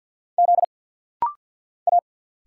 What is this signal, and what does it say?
Morse code, heard as a keyed tone, about 49 wpm, 700 Hz. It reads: Z A
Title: Single Letter-Number Alt - Rapid-Fire 50wpm